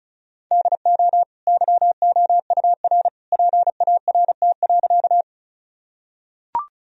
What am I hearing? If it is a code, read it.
Morse code, heard as a keyed tone, about 35 wpm, 700 Hz. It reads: DO YOUR PART.
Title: Sentences from Top 1000 Words - Review Part 1 - 35wpm